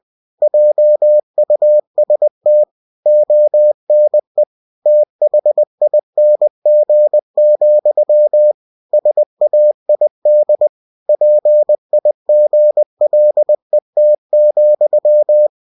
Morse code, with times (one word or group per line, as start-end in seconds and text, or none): 0.4-2.6 JUST
3.1-4.4 ONE
4.9-8.5 THING,
8.9-10.7 SAID
11.1-15.5 PIGLET,